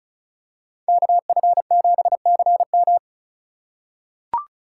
Morse code, heard as a keyed tone, about 36 wpm, 700 Hz. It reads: KF7CM
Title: USA Call Signs - 5 Characters - 3x Character Spacing - 35wpm